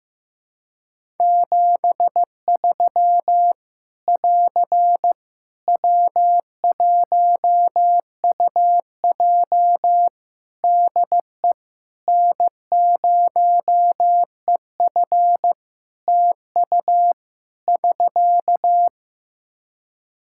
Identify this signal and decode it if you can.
Morse code, heard as a keyed tone, about 15 wpm, 700 Hz. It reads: 73 <AR> W1UJ DE N0EF TU <SK>
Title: Practice QSO - Exchange 4 - Ending - 15wpm